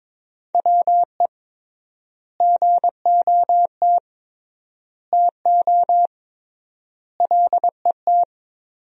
Morse code, heard as a keyed tone, about 22 wpm, 700 Hz. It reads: WE GOT TO LET
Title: 3x Word-Spacing - Sentences from Top 300 Words - 22wpm